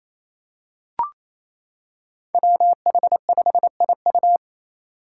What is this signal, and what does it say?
Morse code, heard as a keyed tone, about 27 wpm, 700 Hz. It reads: WH5IU
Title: Five Challenging Letter-Number Combinations 28wpm